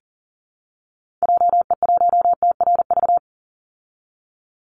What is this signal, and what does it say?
Morse code, heard as a keyed tone, about 40 wpm, 700 Hz. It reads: JE1TRV
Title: Call Signs Repeated 2x - 40wpm